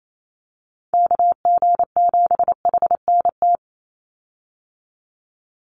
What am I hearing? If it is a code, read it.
Morse code, heard as a keyed tone, about 28 wpm, 700 Hz. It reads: KG7HNT
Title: USA Call Signs 28wpm